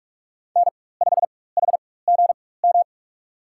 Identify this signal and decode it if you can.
Morse code, heard as a keyed tone, about 47 wpm, spaced as for 21 wpm, 700 Hz. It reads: N5HGM
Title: USA Call Signs - 5 Characters - 4x Character Spacing - 45wpm